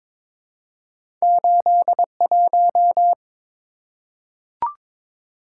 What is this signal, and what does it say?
Morse code, heard as a keyed tone, about 22 wpm, 700 Hz. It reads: 81
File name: practice / 22wpm / Two Letter-Number Combinations - 22wpm